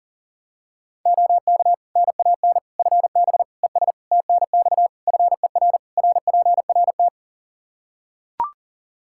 Morse code, heard as a keyed tone, about 40 wpm, 700 Hz. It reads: OK NAN FB ES TNX FER RPRT
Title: Practice QSO - Exchange 2 - Info - 40wpm